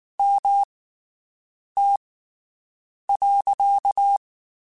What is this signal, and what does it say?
Morse code, heard as a keyed tone, about 19 wpm, spaced as for 7 wpm, 800 Hz. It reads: MT.
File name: (spaced) / QSOs Spaced-mp3s / WD0BGP 1 @18 sp